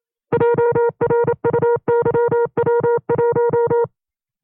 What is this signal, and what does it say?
Morse code, heard as a keyed tone, about 27 wpm, 470 Hz. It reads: JRUYW1
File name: 25WPM